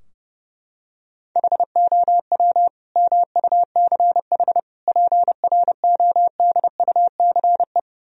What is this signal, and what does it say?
Morse code, heard as a keyed tone, about 30 wpm, 700 Hz. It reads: HOW MUCH PRODUCE
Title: Sentences from Top 1000 Words - Review Part 3 - 30wpm